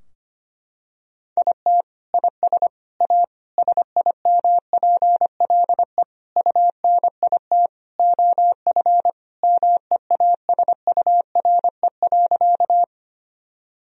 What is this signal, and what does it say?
Morse code, heard as a keyed tone, about 25 wpm, 700 Hz. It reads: IT IS A SIMPLE UNIT OF MEASURE.
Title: Sentences from Top 1000 Words - Review Part 2 - 25wpm